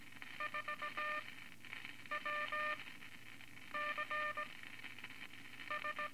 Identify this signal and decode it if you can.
Morse code, heard as a keyed tone, about 17 wpm, spaced as for 7 wpm, 1200 Hz, with noise. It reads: 4WCS
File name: Morse Code Records / 13A LCM 154 Units I-IV Groups of Four cont